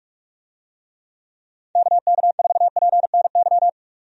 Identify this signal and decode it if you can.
Morse code, heard as a keyed tone, about 41 wpm, 700 Hz. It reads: KK4PNY
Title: USA Call Signs - 6 Characters - 45wpm